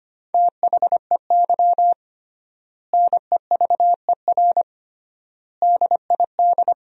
Morse code, heard as a keyed tone, about 25 wpm, 700 Hz. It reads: THEY NEVER DID
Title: 3x Word-Spacing - Sentences from Top 300 Words - 25wpm